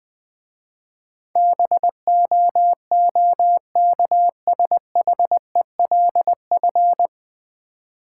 Morse code, heard as a keyed tone, about 20 wpm, 700 Hz. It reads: BOOKSHELF
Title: Easy Words - Speed-Racing with Hints - 20wpm